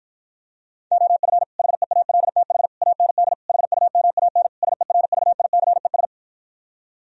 Morse code, heard as a keyed tone, about 52 wpm, 700 Hz. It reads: OF HEALTH AND HUMAN SERVICES